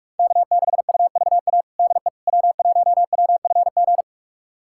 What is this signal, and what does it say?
Morse code, heard as a keyed tone, about 46 wpm, 700 Hz. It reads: K6UVA DE W1WFG
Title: USA Callsign DE Callsign - 5 Characters - 45wpm